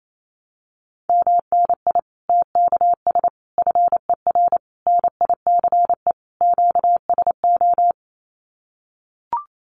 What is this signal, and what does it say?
Morse code, heard as a keyed tone, about 28 wpm, 700 Hz. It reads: MNI TKS FER NICE QSO
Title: Common QSO Phrases 28wpm